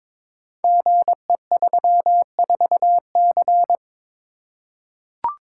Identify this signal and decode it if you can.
Morse code, heard as a keyed tone, about 22 wpm, 700 Hz. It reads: GE34C
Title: Code Groups - 5 Letter:Numbers - 22wpm